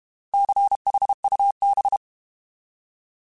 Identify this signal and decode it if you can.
Morse code, heard as a keyed tone, about 32 wpm, 800 Hz. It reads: CHUB
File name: Alphabetical / Alphabetical mp3s / Cs @30